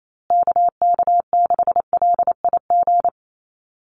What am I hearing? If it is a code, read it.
Morse code, heard as a keyed tone, about 28 wpm, 700 Hz. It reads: KK6LIG
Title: USA Call Signs Repeated 3x 28wpm